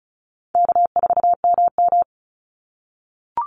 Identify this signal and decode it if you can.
Morse code, heard as a keyed tone, about 34 wpm, 700 Hz. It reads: K4MM
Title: Call Signs - 35wpm